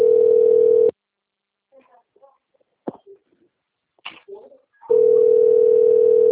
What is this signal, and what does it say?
Ringback tone (North America), 440 + 480 Hz.